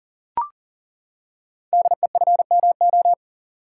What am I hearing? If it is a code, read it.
Morse code, heard as a keyed tone, about 39 wpm, 700 Hz. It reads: DEFMO
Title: Code Groups - 5 Letters - 40wpm